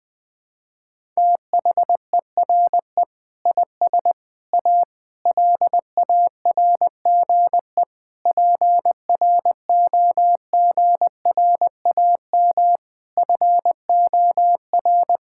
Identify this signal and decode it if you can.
Morse code, heard as a keyed tone, about 20 wpm, 700 Hz. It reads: THERE IS A LARGE PROGRAM FOR
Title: Sentences from Top 1000 Words - Review Part 5 - 20wpm